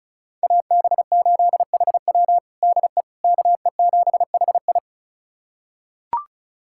Morse code, heard as a keyed tone, about 34 wpm, 700 Hz. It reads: AB8HW DE KE7HI
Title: USA Callsign DE Callsign - 5 Characters - 35wpm